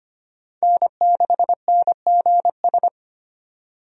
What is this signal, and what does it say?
Morse code, heard as a keyed tone, about 25 wpm, 700 Hz. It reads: N6NGS
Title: USA Call Signs - 5 Characters - 25wpm